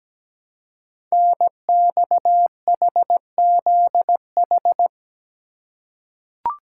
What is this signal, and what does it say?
Morse code, heard as a keyed tone, about 17 wpm, 700 Hz. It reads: NXHZH
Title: Code Groups - 5 Letters - 17wpm